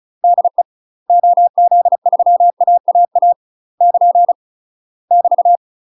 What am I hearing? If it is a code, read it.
Morse code, heard as a keyed tone, about 34 wpm, 700 Hz. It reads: DE OZ3AAA <KN> <BT>
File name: UK MASTER QSOs TRAINING / 35WPM / UK MASTER QSOs 0003